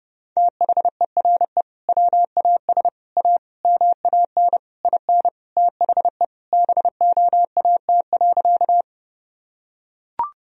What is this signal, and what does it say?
Morse code, heard as a keyed tone, about 30 wpm, 700 Hz. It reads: THERE WAS A MAN IN THE BOAT.
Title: Sentences from Top 1000 Words - Review Part 3 - 30wpm